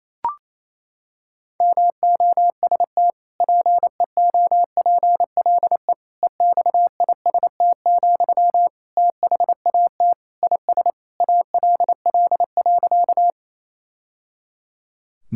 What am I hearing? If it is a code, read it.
Morse code, heard as a keyed tone, about 28 wpm, 700 Hz. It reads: MOST PEOPLE EXIST, THAT IS ALL.
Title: Sentences from Top 1000 Words - Review Part 5 - 28wpm